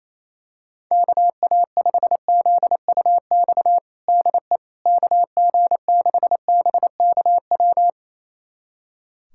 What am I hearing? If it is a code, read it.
Morse code, heard as a keyed tone, about 28 wpm, 700 Hz. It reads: KA5ZUX DE KG6BKW
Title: USA Callsign DE Callsign 28wpm